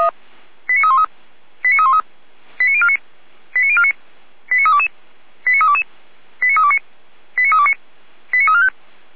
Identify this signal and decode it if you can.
Five-tone selective call (ZVEI), heard as quick runs of five tones, tones of about 70 ms. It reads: 89313 89313 89049 89049 89322 89322 89329 89329 89345